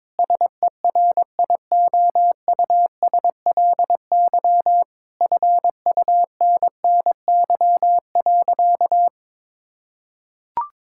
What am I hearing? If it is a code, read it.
Morse code, heard as a keyed tone, about 22 wpm, 700 Hz. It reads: SERIOUSLY FUNNY.
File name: Difficult Sentences with Hints - 22wpm